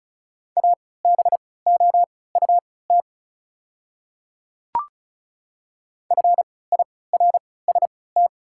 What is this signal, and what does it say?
Morse code, heard as a keyed tone, about 36 wpm, spaced as for 20 wpm, 690 Hz. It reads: ABOUT FIRST
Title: 3x Character Spacing - Top 100 Words - 35wpm